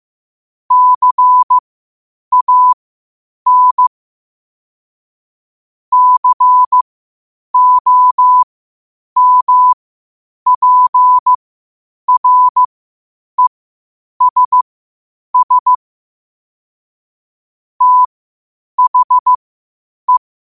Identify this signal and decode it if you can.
Morse code, heard as a keyed tone, about 15 wpm, spaced as for 8 wpm, 1000 Hz. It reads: CAN COMPRESS THE
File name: qotdfiles / 8wpm